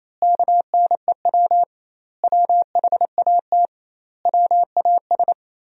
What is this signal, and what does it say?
Morse code, heard as a keyed tone, about 28 wpm, 700 Hz. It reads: KNEW WHAT WAS
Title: 2x Word-Spacing - Sentences from Top 400 Words - 28wpm